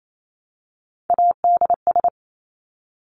Morse code, heard as a keyed tone, about 28 wpm, 700 Hz. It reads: ADS
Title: Abbreviations Repeated 3x - 28wpm